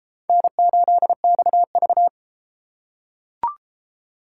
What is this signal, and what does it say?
Morse code, heard as a keyed tone, about 32 wpm, 700 Hz. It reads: N8XV